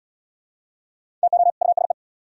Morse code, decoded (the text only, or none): TO USE